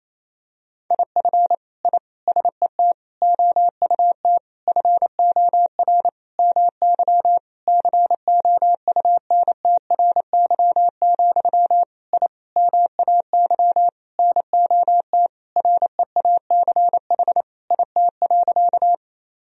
Morse code, heard as a keyed tone, about 28 wpm, 700 Hz. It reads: IF I SET OUT FOR MY COUNTRY, I MAY NOT REACH IT.
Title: Sentences from Top 1000 Words - Review Part 2 - 28wpm